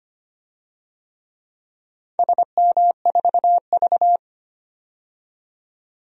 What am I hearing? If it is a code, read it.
Morse code, heard as a keyed tone, about 25 wpm, 700 Hz. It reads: SM4V
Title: DXCC-100 Call Signs - 4 Characters - 25wpm